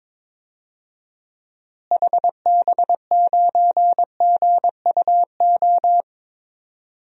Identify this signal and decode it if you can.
Morse code, heard as a keyed tone, about 22 wpm, 700 Hz. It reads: HB9GUO
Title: DXCC-100 Call Signs - 6 Characters - Repeated 3x - 22wpm